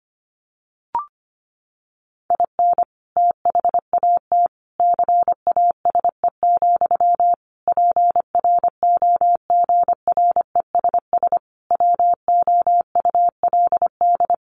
Morse code, heard as a keyed tone, about 25 wpm, 700 Hz. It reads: IN THAT CASE, PROGRESS WOULD
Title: Sentences from Top 1400 Words 25wpm